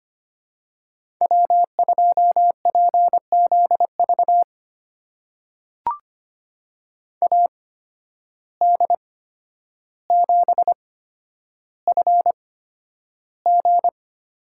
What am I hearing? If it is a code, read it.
Morse code, heard as a keyed tone, about 25 wpm, spaced as for 11 wpm, 700 Hz. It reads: W2PZV AD7FG
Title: USA Call Signs - 5 Characters - 8x Character Spacing - 25wpm